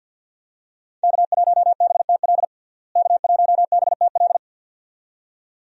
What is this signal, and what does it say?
Morse code, heard as a keyed tone, about 50 wpm, 700 Hz. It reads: K1BTL K1BTL
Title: USA Call Signs - 5 Characters - Repeated 2x - 50wpm